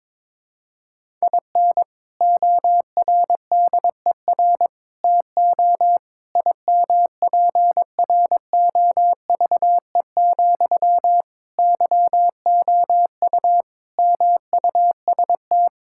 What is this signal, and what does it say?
Morse code, heard as a keyed tone, about 22 wpm, 700 Hz. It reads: IN ORDER TO IMPROVE, YOU MUST
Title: Sentences from Top 1000 Words - Review Part 5 - 22wpm